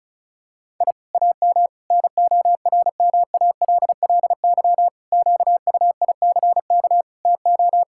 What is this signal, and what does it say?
Morse code, heard as a keyed tone, about 36 wpm, 700 Hz. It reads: I AM NORMALLY QUICK TO
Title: Sentences from Top 1000 Words - Review Part 6 - 35wpm